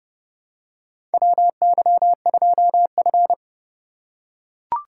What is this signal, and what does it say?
Morse code, heard as a keyed tone, about 30 wpm, 700 Hz. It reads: WY2F
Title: USA Call Signs - 4 Characters - 3x Character Spacing - 30wpm